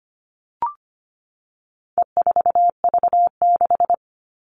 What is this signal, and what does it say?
Morse code, heard as a keyed tone, about 25 wpm, 700 Hz. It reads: E4V6
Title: All Cut Numbers - 4-Digit 25wpm